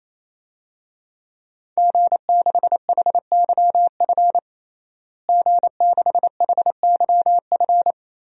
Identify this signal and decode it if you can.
Morse code, heard as a keyed tone, about 27 wpm, 700 Hz. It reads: G6HYF G6HYF
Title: DXCC-100 Call Signs - 5 Characters - Repeated 3x - 28wpm